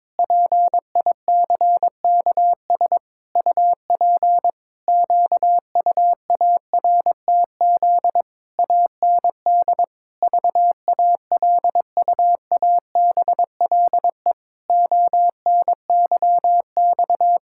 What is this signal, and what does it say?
Morse code, heard as a keyed tone, about 22 wpm, 700 Hz. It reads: PICKS UP QUARTZ AND VALUABLE ONYX
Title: Pangram Challenge v2 - 22wpm